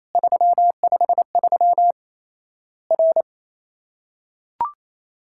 Morse code, heard as a keyed tone, about 28 wpm, 700 Hz. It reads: 353 R
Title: Worked All Europe DX Contest - QTC Exchange - 28wpm